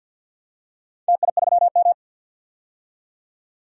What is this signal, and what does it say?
Morse code, heard as a keyed tone, about 51 wpm, 700 Hz. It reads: TI3M
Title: DXCC-100 Call Signs - 4 Characters - 50wpm